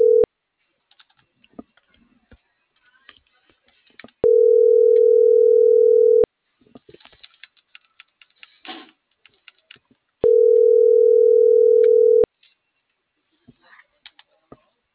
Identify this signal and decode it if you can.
Ringback tone (North America), 440 + 480 Hz.